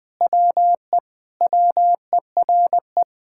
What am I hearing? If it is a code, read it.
Morse code, heard as a keyed tone, about 20 wpm, 700 Hz. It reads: WE WERE
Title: Sentences with 5-Word Phrases - Spoken Beginning - 20wpm